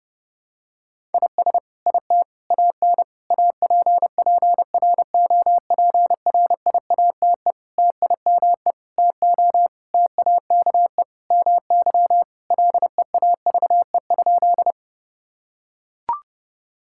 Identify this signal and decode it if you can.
Morse code, heard as a keyed tone, about 30 wpm, 700 Hz. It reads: IS IT AN APPROPRIATE TIME TO TAKE MY LEAVE?
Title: Sentences from Top 1000 Words - Review Part 5 - 30wpm